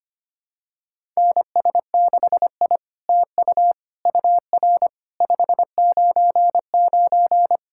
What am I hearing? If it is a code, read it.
Morse code, heard as a keyed tone, about 25 wpm, 700 Hz. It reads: NS6I TU UR 599
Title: SOTA QSO Practice 25wpm